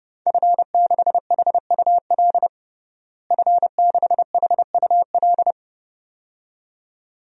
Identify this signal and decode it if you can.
Morse code, heard as a keyed tone, about 32 wpm, 700 Hz. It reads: F6HUL F6HUL
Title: DXCC-100 Call Signs - 5 Characters - Repeated 2x - 30wpm